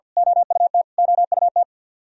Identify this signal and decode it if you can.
Morse code, heard as a keyed tone, about 51 wpm, 700 Hz. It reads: OUT OUT
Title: Top 100 Words Repeated 50wpm